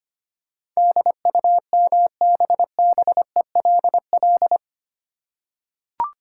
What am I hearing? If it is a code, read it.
Morse code, heard as a keyed tone, about 25 wpm, 700 Hz. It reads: DUMBBELL
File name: practice / 25wpm / Easy Words with Hints - 25wpm